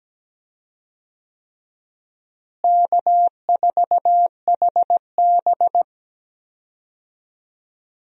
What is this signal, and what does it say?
Morse code, heard as a keyed tone, about 17 wpm, 700 Hz. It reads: K4HB
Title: USA Call Signs - 4 Characters - Repeated 3x - 17wpm